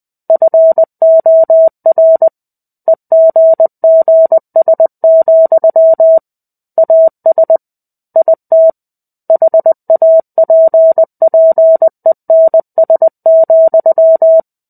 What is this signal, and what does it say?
Morse code, heard as a keyed tone, about 20 wpm, 650 Hz. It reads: FOR EGGS, AS IT HAPPENS,
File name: 05-Chapter-05 20 WPM